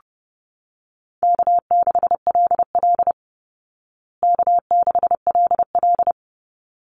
Morse code, heard as a keyed tone, about 30 wpm, 700 Hz. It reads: K6LL K6LL
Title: Call Signs Repeated 3x - 30wpm